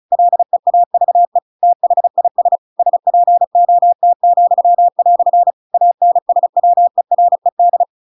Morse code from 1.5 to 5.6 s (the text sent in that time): THIS SPOT,"